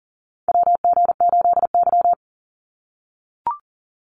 Morse code, heard as a keyed tone, about 40 wpm, 700 Hz. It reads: WG8Y
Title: NA SOTA Chaser Callsigns 40wpm